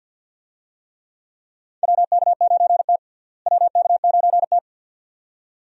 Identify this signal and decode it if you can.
Morse code, heard as a keyed tone, about 46 wpm, 700 Hz. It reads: WK9T WK9T
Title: USA Call Signs - 4 Characters - Repeated 2x - 50wpm